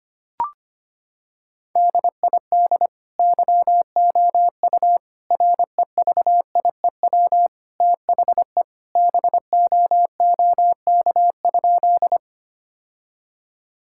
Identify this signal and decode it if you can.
Morse code, heard as a keyed tone, about 25 wpm, 700 Hz. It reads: DID YOU REVIEW THE BOOK?
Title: Sentences from Top 1000 Words - Review Part 6 - 25wpm